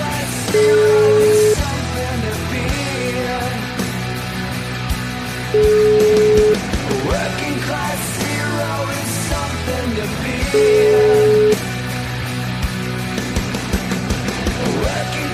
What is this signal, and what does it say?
Ringback tone (Europe, 425 Hz), with noise.